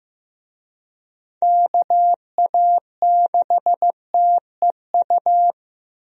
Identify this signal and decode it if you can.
Morse code, heard as a keyed tone, about 16 wpm, 700 Hz. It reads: KA6TEU
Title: USA Call Signs - 6 Characters - Repeated 2x - 15wpm